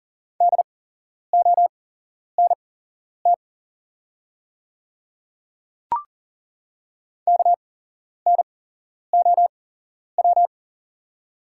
Morse code, heard as a keyed tone, about 40 wpm, spaced as for 11 wpm, 690 Hz. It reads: DONT KNOW
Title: 8x Character Spacing - Top 300 Words - 40wpm